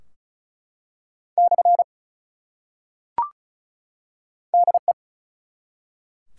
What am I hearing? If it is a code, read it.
Morse code, heard as a keyed tone, about 36 wpm, 700 Hz. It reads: / DE
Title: Top 180 QSO Elements 35wpm